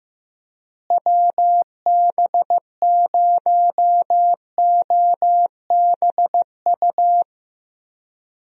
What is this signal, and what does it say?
Morse code, heard as a keyed tone, about 15 wpm, 700 Hz. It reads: WB0OBU
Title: USA Call Signs - 6 Characters - Repeated 3x - 15wpm